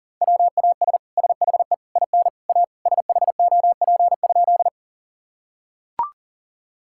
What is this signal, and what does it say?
Morse code, heard as a keyed tone, about 40 wpm, 700 Hz. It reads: WAS SHE IN A SHOP?
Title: Sentences from Top 1600 Words 40wpm